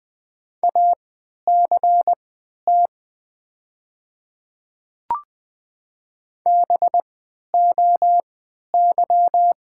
Morse code, heard as a keyed tone, about 20 wpm, spaced as for 11 wpm, 700 Hz. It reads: ACT BOY
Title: 3x Character Spacing - Top 200 Words - 20wpm